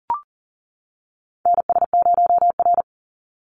Morse code, heard as a keyed tone, about 40 wpm, 700 Hz. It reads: NS0R
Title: Call Signs - 40wpm